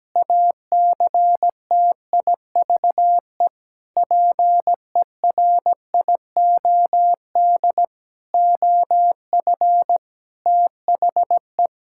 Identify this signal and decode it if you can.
Morse code, heard as a keyed tone, about 17 wpm, 700 Hz. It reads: ACTIVE PERIOD OF THE
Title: Sentences from Top 1000 Words - Review Part 6 - 17wpm